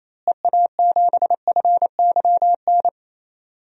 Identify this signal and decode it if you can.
Morse code, heard as a keyed tone, about 28 wpm, 700 Hz. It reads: EA7FYN
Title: DXCC-100 Call Signs - 6 Characters - Repeated 2x - 28wpm